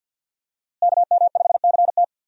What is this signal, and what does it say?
Morse code, heard as a keyed tone, about 45 wpm, 700 Hz. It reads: KM5CT
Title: USA Call Signs - 5 Characters - Repeated 2x - 50wpm